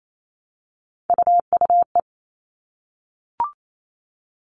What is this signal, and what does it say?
Morse code, heard as a keyed tone, about 28 wpm, 700 Hz. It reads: UUE
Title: All Cut Numbers - 3-Digit 28wpm